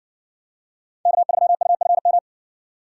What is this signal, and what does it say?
Morse code, heard as a keyed tone, about 58 wpm, 700 Hz. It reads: K3UFM